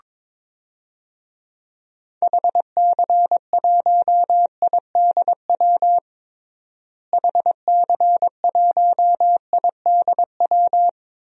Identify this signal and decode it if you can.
Morse code, heard as a keyed tone, about 22 wpm, 700 Hz. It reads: HC1IDW HC1IDW
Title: DXCC-100 Call Signs - 6 Characters - Repeated 3x - 22wpm